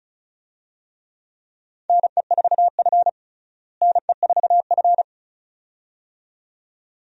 Morse code, heard as a keyed tone, about 36 wpm, 700 Hz. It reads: NE4F NE4F
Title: USA Call Signs - 4 Characters - Repeated 2x - 35wpm